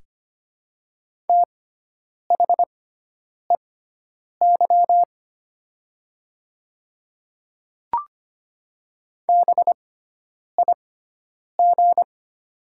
Morse code, heard as a keyed tone, about 25 wpm, spaced as for 9 wpm, 700 Hz. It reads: THEY BIG
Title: 6x Character Spacing - Top 200 Words - 25wpm